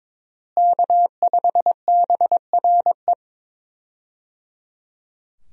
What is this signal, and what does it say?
Morse code, heard as a keyed tone, about 22 wpm, 700 Hz. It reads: K5BRE
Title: USA Call Signs - 5 Characters - Repeated 3x - 22wpm